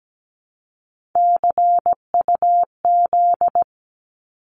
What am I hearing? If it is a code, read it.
Morse code, heard as a keyed tone, about 17 wpm, 700 Hz. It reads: CUZ
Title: Abbreviations 17wpm